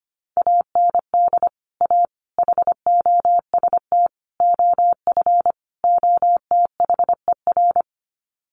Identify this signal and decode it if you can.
Morse code, heard as a keyed tone, about 25 wpm, 700 Hz. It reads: AND A HOST OF OTHER
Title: Sets of 5 Words Encore2 25wpm